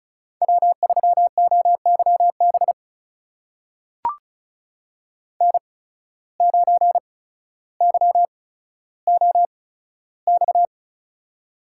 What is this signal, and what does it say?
Morse code, heard as a keyed tone, about 36 wpm, spaced as for 15 wpm, 700 Hz. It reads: W3OYB N9YOX
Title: USA Call Signs - 5 Characters - 8x Character Spacing - 35wpm